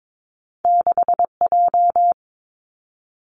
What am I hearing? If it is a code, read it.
Morse code, heard as a keyed tone, about 22 wpm, 700 Hz. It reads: 6J
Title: Two Challenging Letter-Number Combinations 22wpm